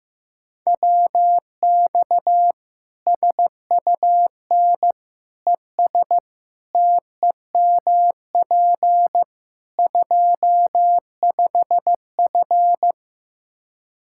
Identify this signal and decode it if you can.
Morse code, heard as a keyed tone, about 16 wpm, 700 Hz. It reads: WX SUN ES TEMP 25F
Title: Practice QSO - Exchange 2 - Info - 15wpm